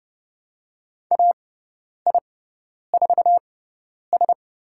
Morse code, heard as a keyed tone, about 31 wpm, spaced as for 10 wpm, 700 Hz. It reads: AI4S
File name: Lesson 18g - Letter m - Call Signs 30wpm